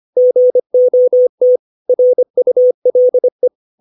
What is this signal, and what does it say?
Morse code, heard as a keyed tone, about 25 wpm, 500 Hz. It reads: GOT RULE
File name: 632 words / 632 Words 25WPM